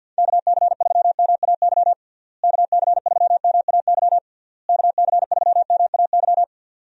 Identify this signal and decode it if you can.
Morse code, heard as a keyed tone, about 50 wpm, 700 Hz. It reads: KC3MAY KC3MAY KC3MAY
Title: USA Call Signs - 6 Characters - Repeated 3x - 50wpm